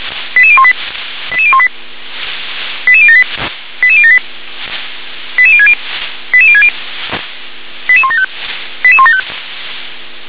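Five-tone selective call (ZVEI), heard as quick runs of five tones, tones of about 70 ms. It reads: 88018 88018 88078 88078 88060 88060 88175 88175
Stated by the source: ZVEI five-tone call received by SDR.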